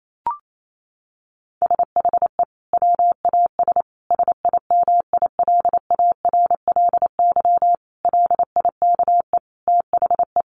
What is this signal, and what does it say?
Morse code, heard as a keyed tone, about 28 wpm, 700 Hz. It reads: SHE WAS SIMILARLY LIKE THE
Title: Sentences from Top 1400 Words 28wpm